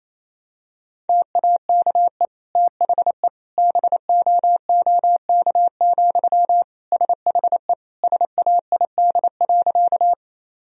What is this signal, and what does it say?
Morse code, heard as a keyed tone, about 28 wpm, 700 Hz. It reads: TAKE THE BOOK, SHE SAID.
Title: Sentences from Top 1000 Words - Review Part 1 - 28wpm